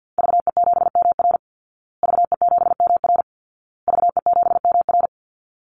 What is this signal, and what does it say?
Morse code, heard as a keyed tone, about 50 wpm, 700 Hz. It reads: VE7MR VE7MR VE7MR
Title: Call Signs Repeated 2x - 50wpm